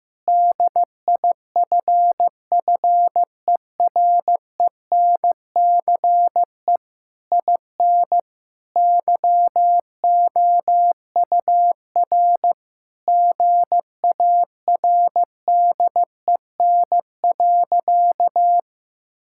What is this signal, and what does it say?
Morse code, heard as a keyed tone, about 16 wpm, 700 Hz. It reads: DIFFERENCE IN YOUR GARDEN.
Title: Sentences from Top 1000 Words - Review Part 5 - 15wpm